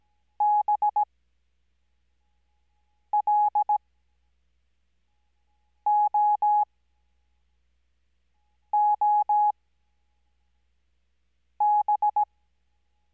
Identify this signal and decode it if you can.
Morse code, heard as a keyed tone, about 17 wpm, spaced as for 4 wpm, 840 Hz. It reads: BLOOB
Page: Farnsworth Spacing: 15WPM Character speed Spaced at 5WPM.